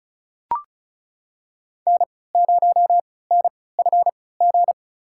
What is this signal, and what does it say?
Morse code, heard as a keyed tone, about 34 wpm, spaced as for 20 wpm, 700 Hz. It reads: N0NFG
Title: USA Call Signs - 5 Characters - 3x Character Spacing - 35wpm